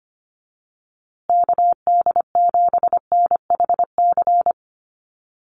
Morse code, heard as a keyed tone, about 25 wpm, 700 Hz. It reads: KD7NHC
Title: USA Call Signs Repeated 3x 25wpm